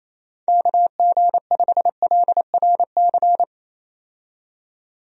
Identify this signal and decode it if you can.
Morse code, heard as a keyed tone, about 27 wpm, 700 Hz. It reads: KG5LRC
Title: USA Call Signs - 6 Characters - 28wpm